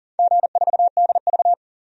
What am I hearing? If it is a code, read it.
Morse code, heard as a keyed tone, about 40 wpm, 700 Hz. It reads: G4DV
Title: DXCC-100 Call Signs - 4 Characters - 40wpm